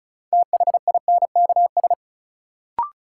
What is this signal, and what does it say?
Morse code, heard as a keyed tone, about 36 wpm, 700 Hz. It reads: THINKS
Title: Sentences from Top 1000 Words - Review Part 6 - New Word Drill - 35wpm